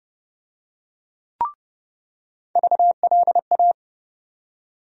Morse code, heard as a keyed tone, about 31 wpm, 700 Hz. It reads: VLA
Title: USA Call Sign Postfixes - 3 Letters 30wpm